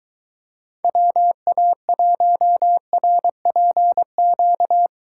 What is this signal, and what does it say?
Morse code, heard as a keyed tone, about 22 wpm, 700 Hz. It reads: WA1RPQ